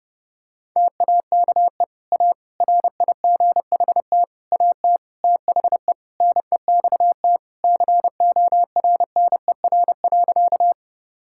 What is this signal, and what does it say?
Morse code, heard as a keyed tone, about 30 wpm, 700 Hz. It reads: TAKE A RIGHT AT THE NEXT CORNER.
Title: Sentences from Top 1600 Words 30wpm